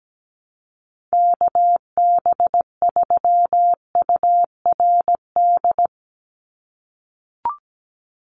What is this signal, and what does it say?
Morse code, heard as a keyed tone, about 17 wpm, 700 Hz. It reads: KB3URD
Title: USA Call Signs 17wpm